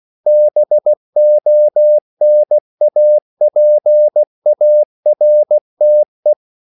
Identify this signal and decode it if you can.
Morse code, heard as a keyed tone, about 16 wpm, 600 Hz. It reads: BONAPARTE
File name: CW / Practice / EnglishQuotations0116